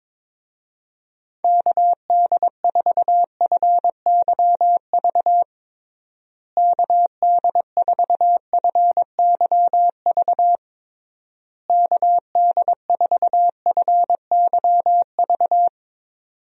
Morse code, heard as a keyed tone, about 22 wpm, 700 Hz. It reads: KD4FYV KD4FYV KD4FYV
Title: USA Call Signs - 6 Characters - Repeated 3x - 22wpm